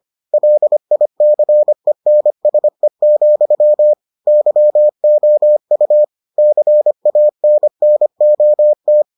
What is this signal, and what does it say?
Morse code, heard as a keyed tone, about 25 wpm, 600 Hz. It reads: LICENSE, YOU CANNOT